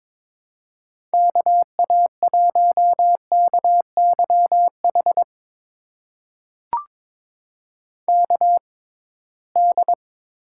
Morse code, heard as a keyed tone, about 22 wpm, spaced as for 17 wpm, 700 Hz. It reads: KA1KYH KD
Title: USA Call Signs - 6 Characters - 6x Character Spacing - 22wpm